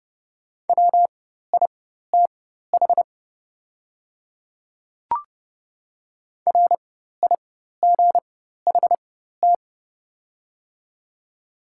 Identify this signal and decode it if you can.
Morse code, heard as a keyed tone, about 30 wpm, spaced as for 14 wpm, 700 Hz. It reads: WITH RIGHT
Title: 4x Character Spacing - Top 300 Words - 30wpm